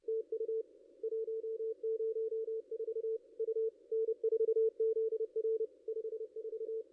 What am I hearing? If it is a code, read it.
Morse code, heard as a keyed tone, about 28 wpm, 450 Hz. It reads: TU 104 U N4ZR 54
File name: contests / ss-cw-k5tr-op-n5rz -2200